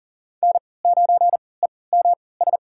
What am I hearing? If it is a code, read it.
Morse code, heard as a keyed tone, about 39 wpm, spaced as for 22 wpm, 700 Hz. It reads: N9EMS
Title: USA Call Signs - 3x Character Spacing - 40wpm